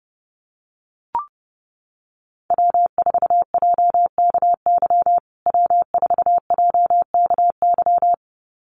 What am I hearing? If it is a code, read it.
Morse code, heard as a keyed tone, about 30 wpm, 700 Hz. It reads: W4JKY W4JKY
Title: USA Call Signs Repeated 2x 30wpm